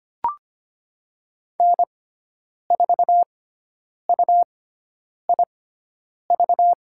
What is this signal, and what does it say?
Morse code, heard as a keyed tone, about 25 wpm, spaced as for 9 wpm, 700 Hz. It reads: N4UIV
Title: USA Call Signs - 6x Character Spacing - 25wpm